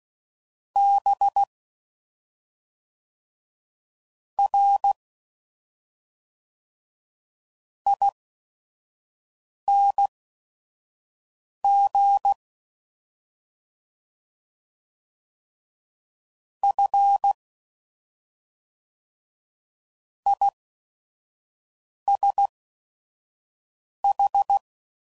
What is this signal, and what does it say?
Morse code, heard as a keyed tone, about 16 wpm, spaced as for 5 wpm, 800 Hz. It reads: B R ING F ISH